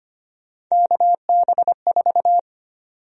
Morse code, heard as a keyed tone, about 25 wpm, 700 Hz. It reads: KB4
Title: USA Call Sign Prefixes - 25wpm